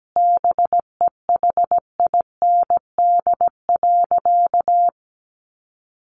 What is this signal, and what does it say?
Morse code, heard as a keyed tone, about 17 wpm, 700 Hz. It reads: BEHIND.
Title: A Sound Of Thunder 17wpm